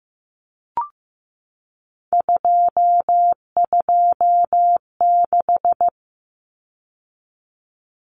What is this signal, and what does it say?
Morse code, heard as a keyed tone, about 15 wpm, 700 Hz. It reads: 226
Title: Numbers - 3-Digit 15wpm